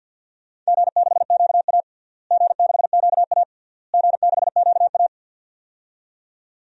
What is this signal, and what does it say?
Morse code, heard as a keyed tone, about 51 wpm, 700 Hz. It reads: G6QA G6QA G6QA
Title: DXCC-100 Call Signs - 4 Characters - Repeated 3x - 50wpm